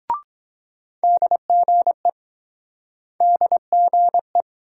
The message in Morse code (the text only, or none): DGE DGE